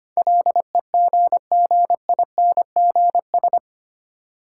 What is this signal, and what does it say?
Morse code, heard as a keyed tone, about 25 wpm, 700 Hz. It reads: LEGGINGS
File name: practice / 25wpm / Difficult Words with Hints - 25wpm